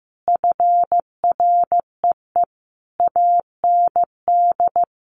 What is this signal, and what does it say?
Morse code, heard as a keyed tone, about 15 wpm, 700 Hz. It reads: FREE AND
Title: Sentences from Top 1500 Words 15wpm